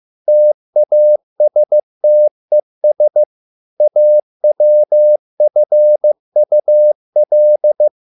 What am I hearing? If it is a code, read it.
Morse code, heard as a keyed tone, about 15 wpm, 600 Hz. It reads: TASTES AWFUL